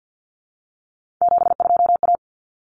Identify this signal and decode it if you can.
Morse code, heard as a keyed tone, about 50 wpm, 700 Hz. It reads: 72A
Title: Common Cut Numbers - 3-Digit 50wpm